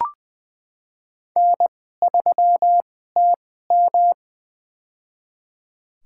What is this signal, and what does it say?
Morse code, heard as a keyed tone, about 20 wpm, spaced as for 14 wpm, 700 Hz. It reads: N3TM
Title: USA Call Signs - 4 Characters - 2x Character Spacing - 20wpm